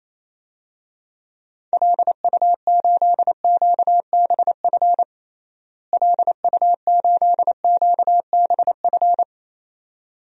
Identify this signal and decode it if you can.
Morse code, heard as a keyed tone, about 28 wpm, 700 Hz. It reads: LU8QBF LU8QBF
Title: DXCC-100 Call Signs - 6 Characters - Repeated 2x - 28wpm